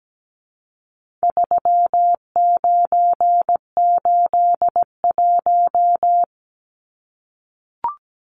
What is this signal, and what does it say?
Morse code, heard as a keyed tone, about 17 wpm, 700 Hz. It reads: 3981
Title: Numbers - 4-Digit 17wpm